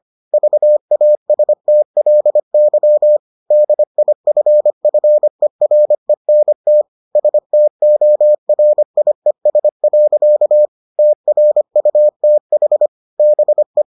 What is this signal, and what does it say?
Morse code, heard as a keyed tone, about 25 wpm, 600 Hz. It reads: VASTLY DIFFERENT STORIES. TRUTH BE